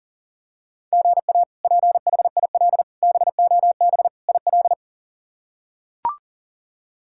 Morse code, heard as a keyed tone, about 40 wpm, 700 Hz. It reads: GA PHIL BOB IL
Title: K1USN SST QSO Practice - 40wpm